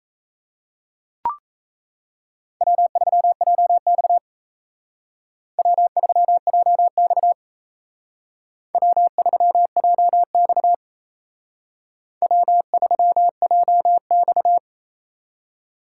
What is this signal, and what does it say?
Morse code, heard as a keyed tone, about 32 wpm, 700 Hz. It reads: W3JX W3JX W3JX W3JX
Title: Speed Racing - USA 1x2 Call Signs 28wpm